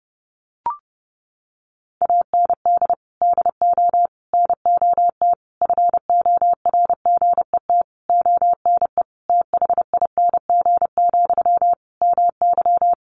Morse code, heard as a keyed tone, about 30 wpm, 700 Hz. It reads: AND DO NOT FORGET ONE THING, MY
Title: Sentences from Top 1200 Words 30wpm